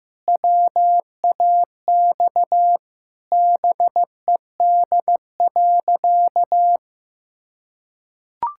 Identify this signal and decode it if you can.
Morse code, heard as a keyed tone, about 16 wpm, 700 Hz. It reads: WAX BED.
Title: Pangram Challenge v2 - 15wpm